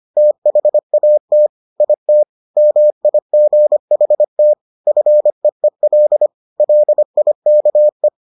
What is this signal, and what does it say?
Morse code, heard as a keyed tone, about 25 wpm, 600 Hz. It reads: THAT IT MIGHT FEEL LIKE